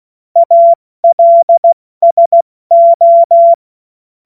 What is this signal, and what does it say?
Morse code, heard as a keyed tone, about 16 wpm, 680 Hz. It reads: ALSO
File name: Morsekurs / Witze / Witz 14. Juni 2022 70 F80